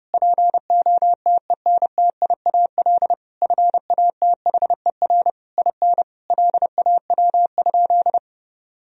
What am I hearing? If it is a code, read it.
Morse code, heard as a keyed tone, about 28 wpm, 700 Hz. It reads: POTENTIAL FATHER IN LAW?
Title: Sentences from Top 1000 Words - Review Part 4 - 30wpm